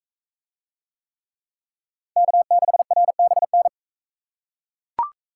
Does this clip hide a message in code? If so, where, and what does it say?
Yes: Morse code from 2.2 to 3.7 s, heard as a keyed tone, about 39 wpm, 700 Hz. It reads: K6RBN